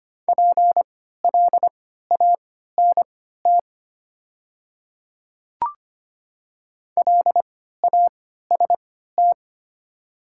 Morse code, heard as a keyed tone, about 25 wpm, spaced as for 14 wpm, 690 Hz. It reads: PLANT LAST
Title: 3x Character Spacing - Top 500 Words - 25wpm